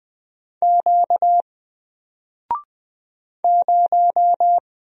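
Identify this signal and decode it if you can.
Morse code, heard as a keyed tone, about 20 wpm, 700 Hz. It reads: Q 0
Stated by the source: Single letter and number Morse drill at 20 wpm, from N4LCD.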